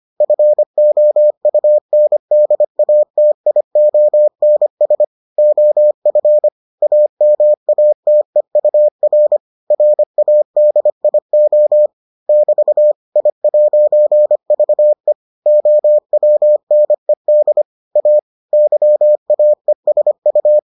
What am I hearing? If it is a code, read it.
Morse code, heard as a keyed tone, about 25 wpm, 600 Hz. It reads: FOUNDATIONS OF AMATEUR RADIO <BT> I'VE OWNED A YAESU